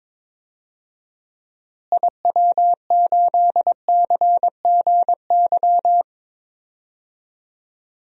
Morse code, heard as a keyed tone, about 22 wpm, 700 Hz. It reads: IW8CGY
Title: DXCC-100 Call Signs - 6 Characters - Repeated 3x - 22wpm